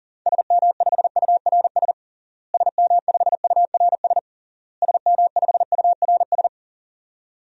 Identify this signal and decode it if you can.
Morse code, heard as a keyed tone, about 40 wpm, 700 Hz. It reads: SM5URS SM5URS SM5URS
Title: DXCC-100 Call Signs - 6 Characters - Repeated 3x - 40wpm